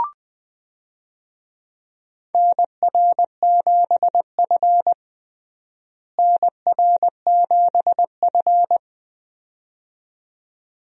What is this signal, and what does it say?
Morse code, heard as a keyed tone, about 21 wpm, 700 Hz. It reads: NR7F NR7F
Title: USA Call Signs - 4 Characters - Repeated 2x - 20wpm